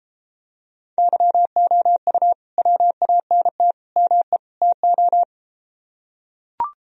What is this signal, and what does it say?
Morse code, heard as a keyed tone, about 33 wpm, 700 Hz. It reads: YOU WANT ME TO